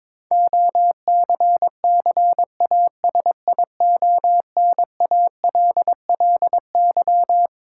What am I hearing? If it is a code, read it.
Morse code, heard as a keyed tone, about 22 wpm, 700 Hz. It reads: OCCASIONALLY
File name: Twelve Letter Words - 22wpm